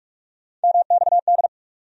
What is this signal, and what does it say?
Morse code, heard as a keyed tone, about 42 wpm, 700 Hz. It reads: MXD
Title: USA Call Sign Postfixes - 3 Letters 45wpm